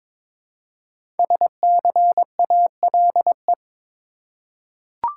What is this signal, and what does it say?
Morse code, heard as a keyed tone, about 22 wpm, 700 Hz. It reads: SCALE
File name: 5 Letter Words - First 3 Letters Given - 22wpm